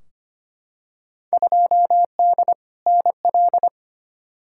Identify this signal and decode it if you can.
Morse code, heard as a keyed tone, about 25 wpm, 700 Hz. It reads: 2D NL
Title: Field Day Exchanges Encore - 25wpm